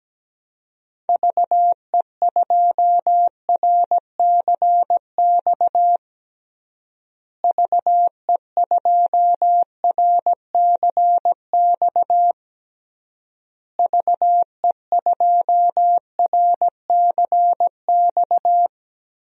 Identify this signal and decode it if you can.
Morse code, heard as a keyed tone, about 17 wpm, 700 Hz. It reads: VE2RCX VE2RCX VE2RCX
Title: DXCC-100 Call Signs - 6 Characters - Repeated 3x - 17wpm